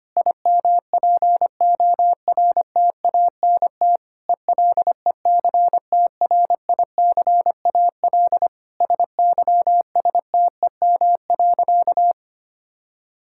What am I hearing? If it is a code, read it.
Morse code, heard as a keyed tone, about 25 wpm, 700 Hz. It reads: IMPORTANT ELECTRICAL SYSTEM.
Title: Sentences from Top 1000 Words - Review Part 6 - 25wpm